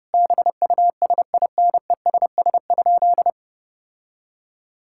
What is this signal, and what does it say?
Morse code, heard as a keyed tone, about 30 wpm, 700 Hz. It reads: BUSINESS?
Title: Sentences from Top 1000 Words - Review Part 5 - 30wpm